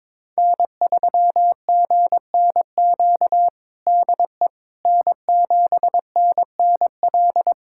Morse code, heard as a keyed tone, about 21 wpm, 700 Hz. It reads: N3GNQ DE N7NNL
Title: USA Callsign DE Callsign - 5 Characters - 22wpm